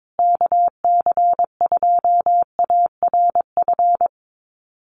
Morse code, heard as a keyed tone, about 22 wpm, 700 Hz. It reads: KC2ARF
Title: USA Call Signs Repeated 2x 22wpm